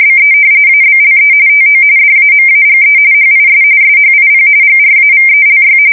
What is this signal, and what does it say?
RTTY, about 45.45 baud. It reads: BE ABLE TO DAMAGE HIS SHIP GRAVELY
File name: fsk170 / battleshipnj